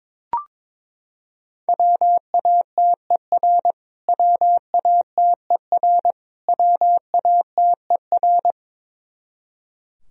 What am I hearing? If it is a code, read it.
Morse code, heard as a keyed tone, about 22 wpm, 700 Hz. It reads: WATER WATER WATER
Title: Top 100 Words Repeated 22wpm